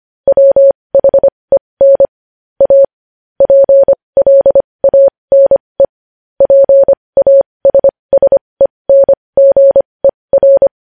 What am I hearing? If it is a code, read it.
Morse code, heard as a keyed tone, about 25 wpm, spaced as for 20 wpm, 550 Hz. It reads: WHEN A PLANE PASSENGER
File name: podcast / News headlines-1x-25x20